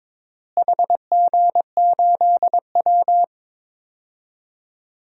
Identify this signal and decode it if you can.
Morse code, heard as a keyed tone, about 22 wpm, 700 Hz. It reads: HG8W
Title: DXCC-100 Call Signs - 4 Characters - Repeated 3x - 22wpm